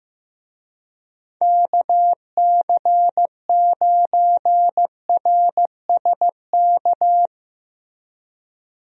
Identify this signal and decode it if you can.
Morse code, heard as a keyed tone, about 15 wpm, 700 Hz. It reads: KC9RSK
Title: USA Call Signs - 6 Characters - Repeated 2x - 15wpm